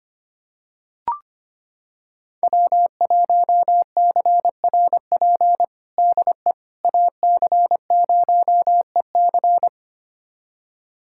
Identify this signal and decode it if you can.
Morse code, heard as a keyed tone, about 25 wpm, 700 Hz. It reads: W1CRP DE AC0EC
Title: USA Callsign DE Callsign - 5 Characters - 25wpm